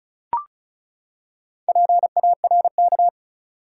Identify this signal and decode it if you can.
Morse code, heard as a keyed tone, about 36 wpm, 700 Hz. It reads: PARK
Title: Four Letter Words - 35wpm